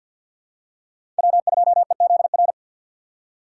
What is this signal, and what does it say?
Morse code, heard as a keyed tone, about 53 wpm, 700 Hz. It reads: W2EZR
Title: USA Call Signs - 5 Characters - 4x Character Spacing - 50wpm